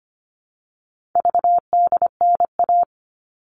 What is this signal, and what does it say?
Morse code, heard as a keyed tone, about 25 wpm, 700 Hz. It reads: VDNA
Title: All Cut Numbers - 4-Digit 25wpm